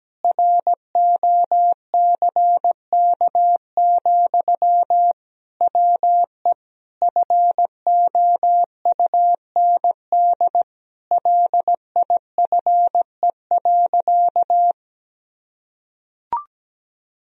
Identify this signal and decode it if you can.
Morse code, heard as a keyed tone, about 17 wpm, 700 Hz. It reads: ROCK, WE FOUND LIFE.
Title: Sentences from Top 1000 Words - Review Part 2 - 17wpm